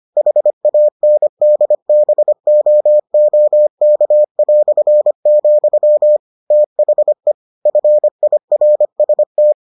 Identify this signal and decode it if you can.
Morse code, heard as a keyed tone, about 25 wpm, 600 Hz. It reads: HANDBOOK", THE FIRST